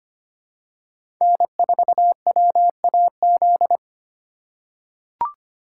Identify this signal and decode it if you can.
Morse code, heard as a keyed tone, about 25 wpm, 700 Hz. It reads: N4WAZ
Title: USA Call Signs - 5 Characters - 3x Character Spacing - 25wpm